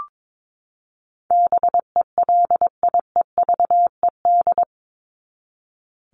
Morse code, heard as a keyed tone, about 22 wpm, 700 Hz. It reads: BELIEVED
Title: ED Suffix - Top 100 Words with Base Words 22wpm